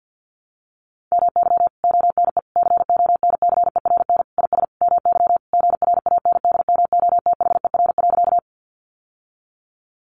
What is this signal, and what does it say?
Morse code, heard as a keyed tone, about 50 wpm, 700 Hz. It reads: MY ONE CONCERN IS MY GRANDMOTHER.
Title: Sentences from Top 1200 Words 50wpm